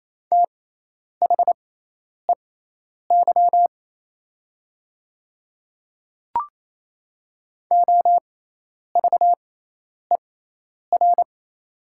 Morse code, heard as a keyed tone, about 28 wpm, spaced as for 10 wpm, 690 Hz. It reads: THEY OVER
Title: 6x Character Spacing - Top 100 Words - 28wpm